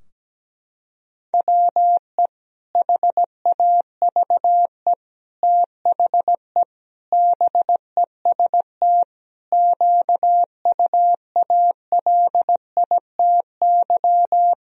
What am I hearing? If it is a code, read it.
Morse code, heard as a keyed tone, about 17 wpm, 700 Hz. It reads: WE HAVE THE BEST QUALITY
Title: Sentences from Top 1000 Words - Review Part 4 - 17wpm